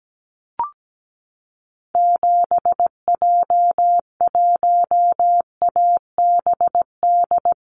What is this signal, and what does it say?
Morse code, heard as a keyed tone, about 17 wpm, 700 Hz. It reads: 7J1ABD
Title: Call Signs Repeated 3x - 17wpm